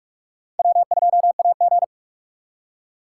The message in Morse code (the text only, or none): W2AG